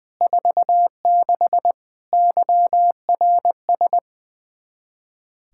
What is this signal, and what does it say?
Morse code, heard as a keyed tone, about 20 wpm, 700 Hz. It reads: 46 YRS
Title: Practice QSO - Exchange 3 - More Info - 20wpm